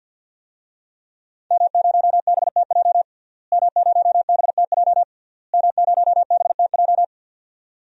Morse code, heard as a keyed tone, about 46 wpm, 700 Hz. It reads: M0BTJ M0BTJ M0BTJ
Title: DXCC-100 Call Signs - 5 Characters - Repeated 3x - 50wpm